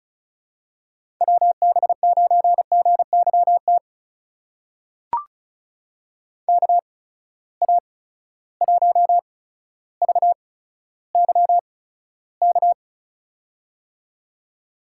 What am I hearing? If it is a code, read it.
Morse code, heard as a keyed tone, about 36 wpm, spaced as for 15 wpm, 700 Hz. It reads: WB9GYT KA1VYK
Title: USA Call Signs - 6 Characters - 8x Character Spacing - 35wpm